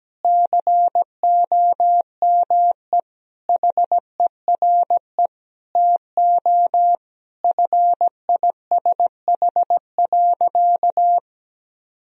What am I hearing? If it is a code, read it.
Morse code, heard as a keyed tone, about 17 wpm, 700 Hz. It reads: COME HERE TO FISH.
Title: Sentences from Top 1000 Words - Review Part 1 - 17wpm